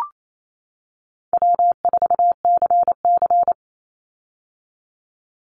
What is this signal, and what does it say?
Morse code, heard as a keyed tone, about 28 wpm, 700 Hz. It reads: W4CC
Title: Call Signs - 28wpm